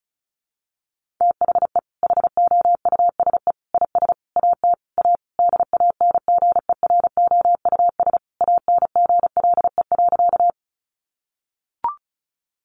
Morse code, heard as a keyed tone, about 34 wpm, 700 Hz. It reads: THE HOUSE IS AT A DANGEROUS ANGLE.
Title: Sentences from Top 1500 Words 35wpm